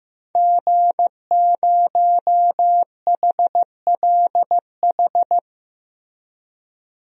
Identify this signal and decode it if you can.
Morse code, heard as a keyed tone, about 15 wpm, 700 Hz. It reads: G0HLH
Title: DXCC-100 Call Signs - 5 Characters - Repeated 3x - 15wpm